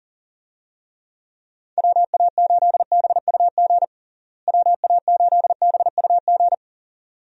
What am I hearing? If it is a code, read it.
Morse code, heard as a keyed tone, about 40 wpm, 700 Hz. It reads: WA8BUG WA8BUG
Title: USA Call Signs - 6 Characters - Repeated 2x - 40wpm